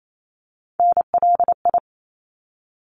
Morse code, heard as a keyed tone, about 28 wpm, 700 Hz. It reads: NLI
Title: ARRL Sections 28wpm